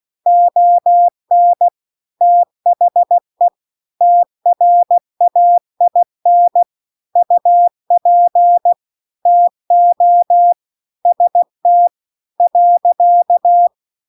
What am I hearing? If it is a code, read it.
Morse code, heard as a keyed tone, about 16 wpm, 700 Hz. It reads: ON THE TRAIN UP TO ST .
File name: CW-The-Big-Twoheaded-River-2-16WPM